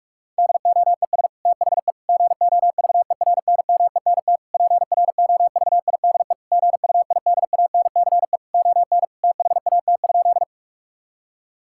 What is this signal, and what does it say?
Morse code, heard as a keyed tone, about 42 wpm, 700 Hz. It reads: DOES THE GOVERNMENT PROVIDE GUIDANCE ON THAT?
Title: Sentences from Top 1600 Words 45wpm